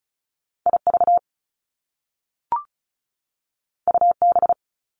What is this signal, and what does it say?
Morse code, heard as a keyed tone, about 36 wpm, 700 Hz. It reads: IV UB
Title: Top Two Letter Combinations in Words 35wpm